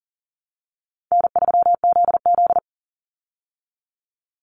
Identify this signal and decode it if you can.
Morse code, heard as a keyed tone, about 37 wpm, 700 Hz. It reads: N3ZZ
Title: Easy Call Signs Repeated 2x - 40wpm